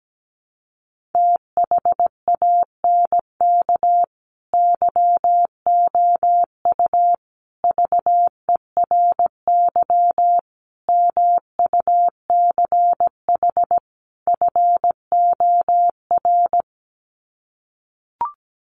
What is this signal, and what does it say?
Morse code, heard as a keyed tone, about 17 wpm, 700 Hz. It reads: THANK YOU VERY MUCH FOR
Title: Sets of 5 Words 17wpm